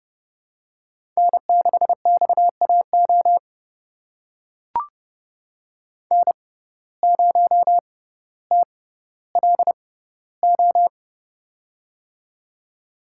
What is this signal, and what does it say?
Morse code, heard as a keyed tone, about 30 wpm, spaced as for 15 wpm, 700 Hz. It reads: N6XAO N0TLO
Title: USA Call Signs - 5 Characters - 6x Character Spacing - 30wpm